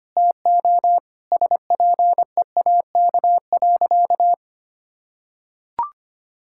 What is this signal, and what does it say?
Morse code, heard as a keyed tone, about 25 wpm, 700 Hz. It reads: TO SPEAK.
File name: practice / 25wpm / Sentences with 3-Word Phrases - 25wpm